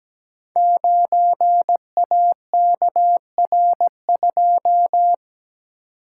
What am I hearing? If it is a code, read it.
Morse code, heard as a keyed tone, about 17 wpm, 700 Hz. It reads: 9AKR2
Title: Code Groups - 5 Letter:Numbers - 17wpm